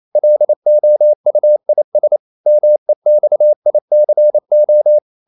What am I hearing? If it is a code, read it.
Morse code, heard as a keyed tone, about 28 wpm, 600 Hz. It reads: LOUIS MEXICO
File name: CW / Practice / NationalCapitals28